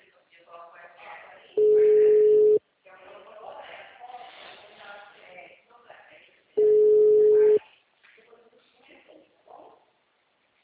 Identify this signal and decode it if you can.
Ringback tone (Europe, 425 Hz), with speech.